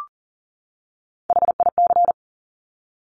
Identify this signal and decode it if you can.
Morse code, heard as a keyed tone, about 39 wpm, 700 Hz. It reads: HIC
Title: Top Three Letter Combinations in Words 40wpm